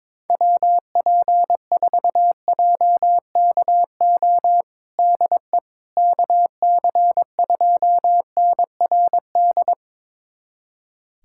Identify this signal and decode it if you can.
Morse code, heard as a keyed tone, about 21 wpm, 700 Hz. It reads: WP4JKO DE KC2NRD
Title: USA Callsign DE Callsign 22wpm